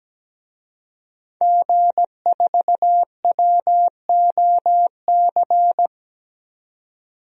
Morse code, heard as a keyed tone, about 17 wpm, 700 Hz. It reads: G4WOC
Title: DXCC-100 Call Signs - 5 Characters - 17wpm